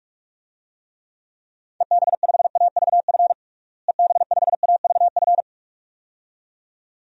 Morse code, heard as a keyed tone, about 46 wpm, 700 Hz. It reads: EB5AVF EB5AVF
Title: DXCC-100 Call Signs - 6 Characters - Repeated 2x - 45wpm